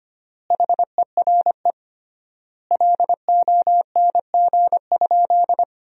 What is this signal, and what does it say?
Morse code, heard as a keyed tone, about 25 wpm, 700 Hz. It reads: HERE LONG?
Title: 3x Word-Spacing - Sentences from Top 200 Words - 25wpm